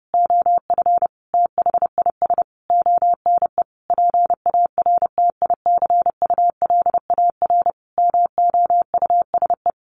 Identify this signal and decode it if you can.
Morse code, heard as a keyed tone, about 30 wpm, 700 Hz. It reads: OF THIS ONE PARTICULAR MOUSE
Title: A Sound Of Thunder 30wpm